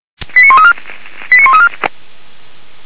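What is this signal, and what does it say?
Five-tone selective call (ZVEI), heard as quick runs of five tones, tones of about 70 ms. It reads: 89134 89134
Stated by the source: ZVEI five-tone selective call received on an SDR.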